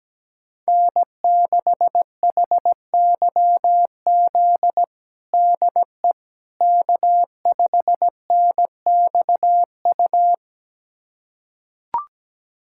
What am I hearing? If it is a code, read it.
Morse code, heard as a keyed tone, about 17 wpm, 700 Hz. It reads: N6HYZ DE K5NXU
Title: USA Callsign DE Callsign - 5 Characters - 17wpm